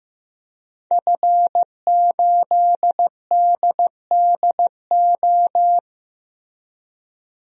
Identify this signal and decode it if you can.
Morse code, heard as a keyed tone, about 15 wpm, 700 Hz. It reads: F8DDO
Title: DXCC-100 Call Signs - 5 Characters - Repeated 3x - 15wpm